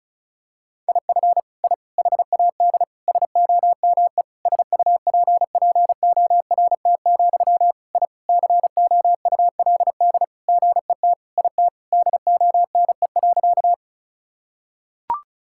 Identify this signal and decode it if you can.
Morse code, heard as a keyed tone, about 34 wpm, 700 Hz. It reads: IF I HAD SOME SUPPORT, I COULD GET IT DONE.
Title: Sentences from Top 1000 Words - Review Part 5 - 35wpm